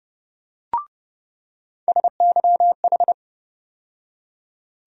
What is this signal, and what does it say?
Morse code, heard as a keyed tone, about 30 wpm, 700 Hz. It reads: SYH
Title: USA Call Sign Postfixes - 3 Letters 30wpm